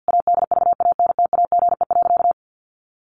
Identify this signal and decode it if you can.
Morse code, heard as a keyed tone, about 50 wpm, 700 Hz. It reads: ADVANTAGE.
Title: Sentences from Top 1400 Words 50wpm